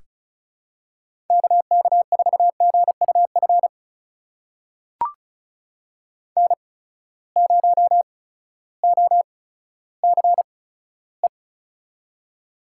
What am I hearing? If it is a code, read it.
Morse code, heard as a keyed tone, about 36 wpm, spaced as for 16 wpm, 700 Hz. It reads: KK4GUF N0OCE
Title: USA Call Signs - 8x Character Spacing - 35wpm